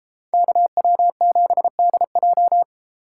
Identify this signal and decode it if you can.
Morse code, heard as a keyed tone, about 33 wpm, 700 Hz. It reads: KW7DJ